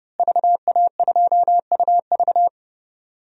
Morse code, heard as a keyed tone, about 30 wpm, 700 Hz. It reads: VA2UV
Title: DXCC-100 Call Signs - 5 Characters - Repeated 2x - 30wpm